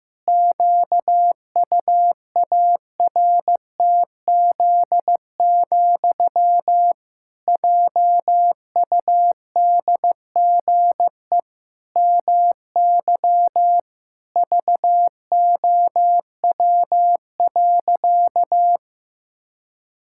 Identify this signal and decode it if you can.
Morse code, heard as a keyed tone, about 15 wpm, 700 Hz. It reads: QUARTZ, JUDGE MY VOW.
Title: Pangram Challenge v2 - 15wpm